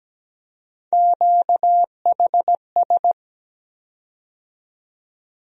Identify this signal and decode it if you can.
Morse code, heard as a keyed tone, about 17 wpm, 700 Hz. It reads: QHS
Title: USA Call Sign Postfixes - 3 Letters 17wpm